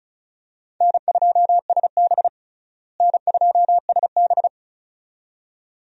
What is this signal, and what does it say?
Morse code, heard as a keyed tone, about 36 wpm, 700 Hz. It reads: N2SB N2SB
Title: USA Call Signs - 4 Characters - Repeated 2x - 35wpm